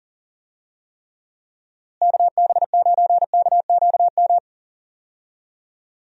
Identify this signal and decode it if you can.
Morse code, heard as a keyed tone, about 40 wpm, 700 Hz. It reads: KB9KQM
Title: USA Call Signs - 6 Characters - Repeated 2x - 40wpm